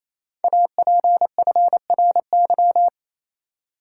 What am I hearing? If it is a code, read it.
Morse code, heard as a keyed tone, about 28 wpm, 700 Hz. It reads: APFRY
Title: Code Groups - 5 Letters - 28wpm